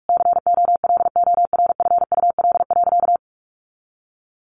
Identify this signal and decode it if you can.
Morse code, heard as a keyed tone, about 44 wpm, 700 Hz. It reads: COLORFUL.
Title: Easy Phrases 45wpm